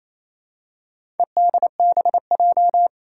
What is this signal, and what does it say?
Morse code, heard as a keyed tone, about 28 wpm, 700 Hz. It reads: EDBJ
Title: Four Challenging Letter-Number Combinations 28wpm